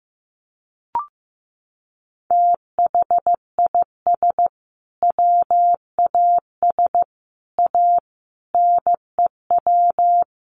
Morse code, heard as a keyed tone, about 14 wpm, 700 Hz. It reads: THIS WAS A NEW
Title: Sentences from Top 1300 Words 15wpm